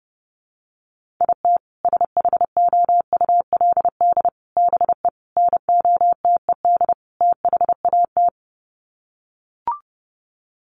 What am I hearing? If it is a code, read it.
Morse code, heard as a keyed tone, about 30 wpm, 700 Hz. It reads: IT SHOULD BE NOTED THAT
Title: Sets of 5 Words 30wpm